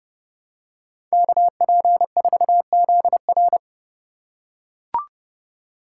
Morse code, heard as a keyed tone, about 30 wpm, 700 Hz. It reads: KP4ZR
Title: USA Call Signs - 5 Characters - 4x Character Spacing - 30wpm